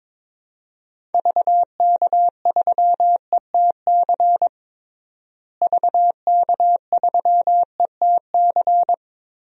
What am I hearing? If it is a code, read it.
Morse code, heard as a keyed tone, about 22 wpm, 700 Hz. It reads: VK3ETC VK3ETC
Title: DXCC-100 Call Signs - 6 Characters - Repeated 3x - 22wpm